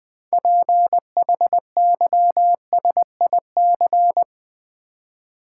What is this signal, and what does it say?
Morse code, heard as a keyed tone, about 21 wpm, 700 Hz. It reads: PHYSIC
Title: Build A Word - 3 Letters At A Time - 8-Letter Words - 20wpm